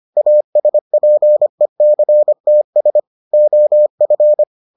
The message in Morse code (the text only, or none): ASPECTS OF